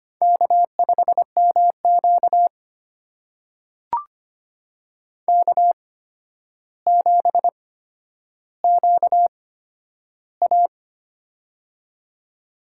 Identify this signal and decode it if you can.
Morse code, heard as a keyed tone, about 25 wpm, spaced as for 11 wpm, 700 Hz. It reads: K5MQ K7QA
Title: USA Call Signs - 4 Characters - 8x Character Spacing - 25wpm